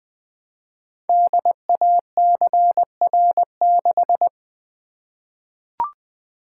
Morse code, heard as a keyed tone, about 20 wpm, 700 Hz. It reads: DACR6
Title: Code Groups - 5 Letter:Numbers - 20wpm